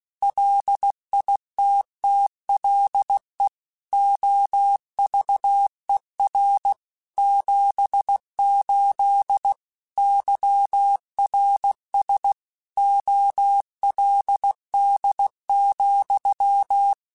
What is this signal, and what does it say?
Morse code, heard as a keyed tone, about 16 wpm, 800 Hz. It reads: LITTLE OVER 78 YRS OLD,
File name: QSOs / QSOs-mp3s / kc7haa@15